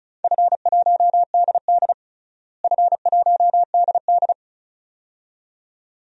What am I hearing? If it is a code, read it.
Morse code, heard as a keyed tone, about 36 wpm, 700 Hz. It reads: F1DD F1DD
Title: DXCC-100 Call Signs - 4 Characters - Repeated 2x - 35wpm